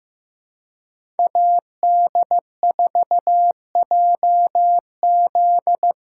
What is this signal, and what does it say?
Morse code, heard as a keyed tone, about 15 wpm, 700 Hz. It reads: AD4JZ
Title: USA Callsign DE Callsign - 5 Characters - 15wpm